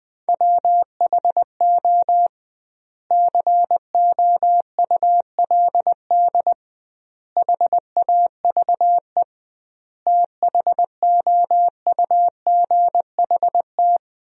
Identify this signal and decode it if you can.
Morse code, heard as a keyed tone, about 21 wpm, 700 Hz. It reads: WHO COULD HAVE THOUGHT
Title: 2x Word-Spacing - Sentences from Top 300 Words - 20wpm